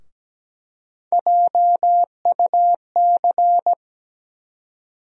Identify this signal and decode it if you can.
Morse code, heard as a keyed tone, about 17 wpm, 700 Hz. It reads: JUC
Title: USA Call Sign Postfixes - 3 Letters 17wpm